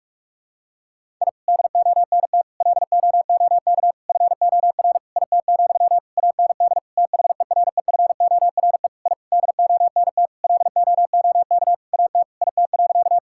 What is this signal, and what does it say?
Morse code, heard as a keyed tone, about 46 wpm, 700 Hz. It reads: I DONT LOOK FOR IT, AND THEREFORE I DONT LOOK AT IT.
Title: Sentences from Top 1000 Words - Review Part 5 - 45wpm